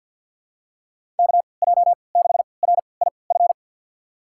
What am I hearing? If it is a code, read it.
Morse code, heard as a keyed tone, about 50 wpm, spaced as for 28 wpm, 700 Hz. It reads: KJ6RIF
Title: USA Call Signs - 6 Characters - 3x Character Spacing - 50wpm